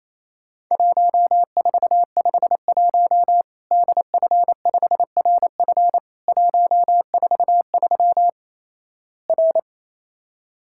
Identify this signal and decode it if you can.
Morse code, heard as a keyed tone, about 28 wpm, 700 Hz. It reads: 1451 DF5RF 143 R
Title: Worked All Europe DX Contest - QTC Exchange - 28wpm